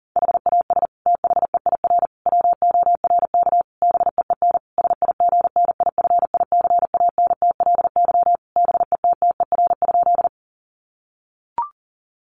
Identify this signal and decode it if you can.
Morse code, heard as a keyed tone, about 40 wpm, 700 Hz. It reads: HAS THEIR WORK BEEN SIGNIFICANTLY BETTER?
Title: Sentences from Top 1500 Words 40wpm